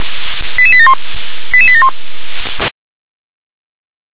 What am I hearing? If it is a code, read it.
Five-tone selective call (ZVEI), heard as quick runs of five tones, tones of about 70 ms. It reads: 88871 88871
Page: /357/SDR/ZVEI/